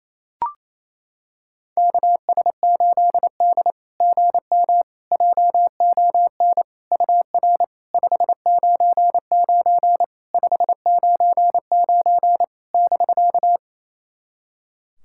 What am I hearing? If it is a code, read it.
Morse code, heard as a keyed tone, about 28 wpm, 690 Hz. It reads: KS8D GM JON UR 599 599 <BK>
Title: SOTA QSO Practice 28wpm